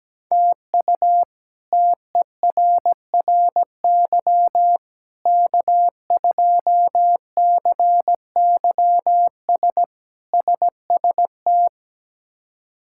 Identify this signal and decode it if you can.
Morse code, heard as a keyed tone, about 17 wpm, 700 Hz. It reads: TU TERRY K2CYS SST
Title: K1USN SST QSO Practice - 17wpm